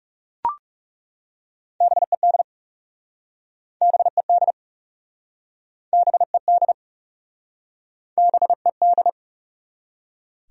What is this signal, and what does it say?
Morse code, heard as a keyed tone, about 40 wpm, 700 Hz. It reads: BED BED BED BED